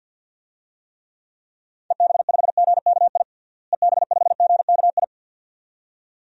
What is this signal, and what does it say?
Morse code, heard as a keyed tone, about 53 wpm, 700 Hz. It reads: EB5GKI EB5GKI
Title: DXCC-100 Call Signs - 6 Characters - Repeated 2x - 50wpm